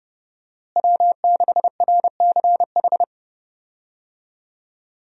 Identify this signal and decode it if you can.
Morse code, heard as a keyed tone, about 30 wpm, 700 Hz. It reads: W6RCH
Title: USA Call Signs - 5 Characters - Repeated 2x - 30wpm